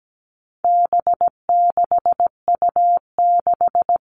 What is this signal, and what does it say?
Morse code, heard as a keyed tone, about 17 wpm, 700 Hz. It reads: B6U6
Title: All Cut Numbers - 4-Digit 17wpm